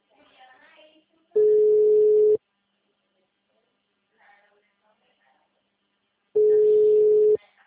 Ringback tone (Europe, 425 Hz).